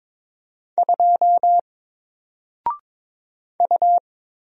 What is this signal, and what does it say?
Morse code, heard as a keyed tone, about 22 wpm, 700 Hz. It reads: 2 U
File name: Single-Letter-Number-22wpm